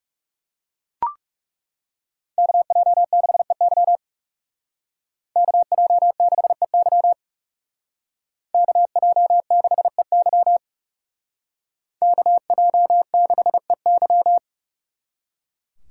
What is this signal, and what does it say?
Morse code, heard as a keyed tone, about 39 wpm, 700 Hz. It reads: KJ6EY KJ6EY KJ6EY KJ6EY